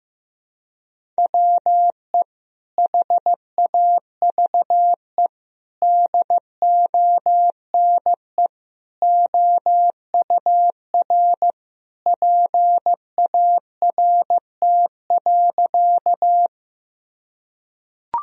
Morse code, heard as a keyed tone, about 15 wpm, 700 Hz. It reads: WE HAVE DONE OUR PART.
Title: Sentences from Top 1000 Words - Review Part 1 - 15wpm